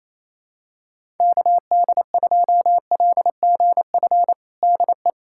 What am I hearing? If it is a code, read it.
Morse code, heard as a keyed tone, about 28 wpm, 700 Hz. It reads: KD2LGF DE
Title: USA Callsign DE Callsign - 6 Characters - 28wpm